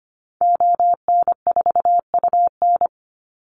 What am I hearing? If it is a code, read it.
Morse code, heard as a keyed tone, about 25 wpm, 700 Hz. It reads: ON4UN
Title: Call Signs Repeated 3x - 25wpm